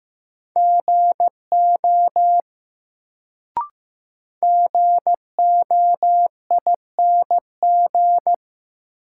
Morse code, heard as a keyed tone, about 15 wpm, 700 Hz. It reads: GO GOING
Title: Sets of 2 Words - New Word Drill - 15wpm